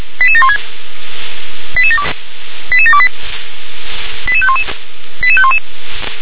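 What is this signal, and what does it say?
Five-tone selective call (ZVEI), heard as quick runs of five tones, tones of about 80 ms. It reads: 88716 88728 88511 88511
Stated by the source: ZVEI five-tone selective call received on an SDR.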